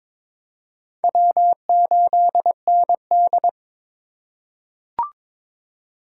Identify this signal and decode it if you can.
Morse code, heard as a keyed tone, about 22 wpm, 700 Hz. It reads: W8ND
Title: USA Call Signs - 4 Characters - 2x Character Spacing - 22wpm